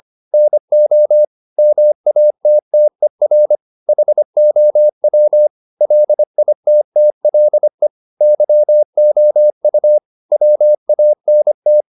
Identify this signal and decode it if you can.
Morse code, heard as a keyed tone, about 25 wpm, 600 Hz. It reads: NO MATTER HOW LITTLE YOU WANT
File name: the old man and the sea part 1 25wpm